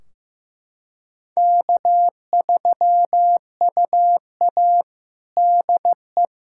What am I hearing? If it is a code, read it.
Morse code, heard as a keyed tone, about 16 wpm, 700 Hz. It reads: K3UA DE
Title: Practice QSO - Exchange 4 - Ending - 15wpm